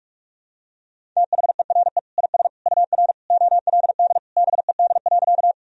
Morse code, heard as a keyed tone, about 46 wpm, 700 Hz. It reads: THERE IS UR OLD BED.
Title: Sentences from Top 500 Words and Abbreviations 45wpm